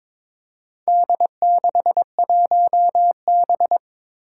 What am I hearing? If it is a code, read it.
Morse code, heard as a keyed tone, about 22 wpm, 700 Hz. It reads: D61B
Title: Four Challenging Letter-Number Combinations 22wpm